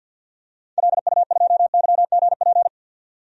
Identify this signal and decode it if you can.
Morse code, heard as a keyed tone, about 52 wpm, 700 Hz. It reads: LU2YGP
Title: DXCC-100 Call Signs - 6 Characters - 50wpm